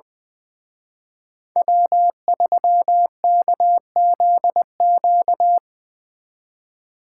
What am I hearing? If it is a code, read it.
Morse code, heard as a keyed tone, about 22 wpm, 700 Hz. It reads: W3KZQ
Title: Speed Racing - USA 1x3 Call Signs 20wpm